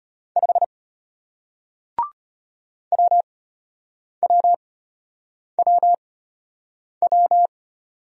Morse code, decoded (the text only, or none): 5 WWWW